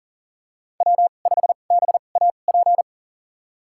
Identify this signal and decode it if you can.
Morse code, heard as a keyed tone, about 40 wpm, spaced as for 29 wpm, 700 Hz. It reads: W5BAP
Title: USA Call Signs - 5 Characters - 2x Character Spacing - 40wpm